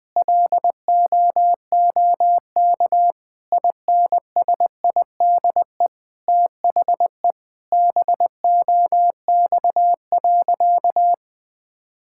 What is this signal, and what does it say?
Morse code, heard as a keyed tone, about 20 wpm, 700 Hz. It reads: LOOK INSIDE THE BOX.
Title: Sentences from Top 1000 Words - Review Part 6 - 20wpm